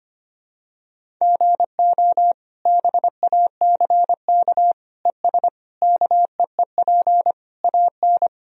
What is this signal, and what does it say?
Morse code, heard as a keyed tone, about 25 wpm, 700 Hz. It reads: GO BACK ES KEEP AN
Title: Sentences from Top 300 Words and Abbreviations 25wpm